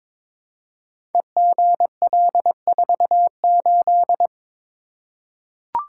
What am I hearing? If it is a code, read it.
Morse code, heard as a keyed tone, about 22 wpm, 690 Hz. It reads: EGL48
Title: Code Groups - 5 Letter:Numbers - 22wpm